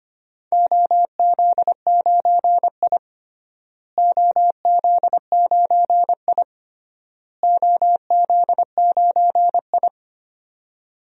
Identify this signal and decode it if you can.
Morse code, heard as a keyed tone, about 25 wpm, 700 Hz. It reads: OZ9I OZ9I OZ9I
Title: DXCC-100 Call Signs - 4 Characters - Repeated 3x - 25wpm